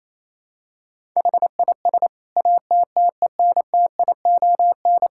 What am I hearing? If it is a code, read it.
Morse code, heard as a keyed tone, about 28 wpm, 700 Hz. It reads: HIS ATTENTION
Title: Sentences from Top 1000 Words - Review Part 6 - 28wpm